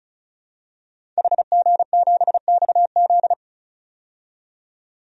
Morse code, heard as a keyed tone, about 36 wpm, 700 Hz. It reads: HG7XZ
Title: DXCC-100 Call Signs - 5 Characters - 35wpm